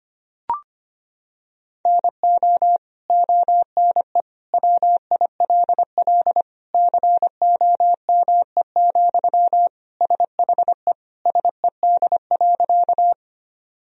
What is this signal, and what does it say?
Morse code, heard as a keyed tone, about 25 wpm, 700 Hz. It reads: NO ONE WILL COME, SHE SED.
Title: Sentences from Top 100 Words and Abbreviations 25wpm